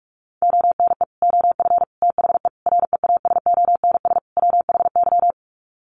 Morse code, heard as a keyed tone, about 44 wpm, 700 Hz. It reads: ONE OF THE REASONS WHY
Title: Sets of 5 Words 45wpm